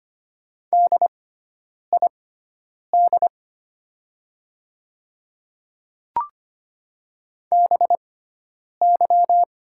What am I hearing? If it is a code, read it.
Morse code, heard as a keyed tone, about 25 wpm, spaced as for 9 wpm, 700 Hz. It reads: DID BY
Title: 6x Character Spacing - Top 200 Words - 25wpm